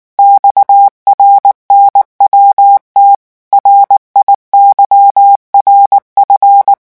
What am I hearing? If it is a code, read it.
Morse code, heard as a keyed tone, about 19 wpm, 800 Hz. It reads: XRNWT RIYRF
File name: morse mp3 / revision7-vitesse 18 mots